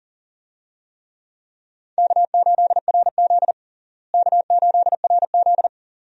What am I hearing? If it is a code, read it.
Morse code, heard as a keyed tone, about 40 wpm, 700 Hz. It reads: K8RZ K8RZ
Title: USA Call Signs - 4 Characters - Repeated 2x - 40wpm